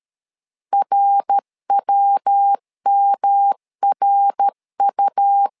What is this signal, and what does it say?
Morse code, heard as a keyed tone, about 13 wpm, 800 Hz. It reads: RWMRU
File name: LEZIONI / 3 MWRO / 3 MWR 12wpm 1711